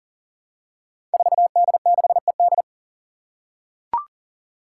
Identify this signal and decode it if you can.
Morse code, heard as a keyed tone, about 40 wpm, 700 Hz. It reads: 4D6ED
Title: Five Challenging Letter-Number Combinations 40wpm